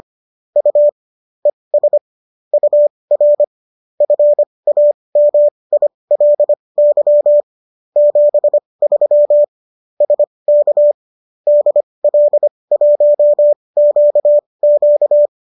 Morse code, heard as a keyed tone, about 25 wpm, spaced as for 20 wpm, 590 Hz. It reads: U ES UR FAMILY 73 SK DL1QQ